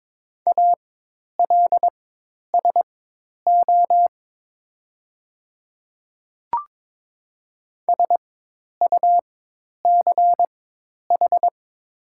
Morse code, heard as a keyed tone, about 22 wpm, spaced as for 10 wpm, 700 Hz. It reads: ALSO SUCH
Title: 4x Character Spacing - Top 200 Words - 22wpm